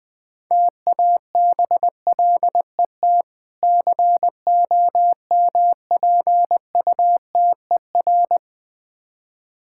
Morse code, heard as a keyed tone, about 20 wpm, 700 Hz. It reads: TABLET COMPUTER
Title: Difficult Words - Speed-Racing with Hints - 20wpm